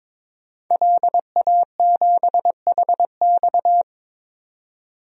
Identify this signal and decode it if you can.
Morse code, heard as a keyed tone, about 22 wpm, 700 Hz. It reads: LA7HX
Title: DXCC-100 Call Signs - 5 Characters - 22wpm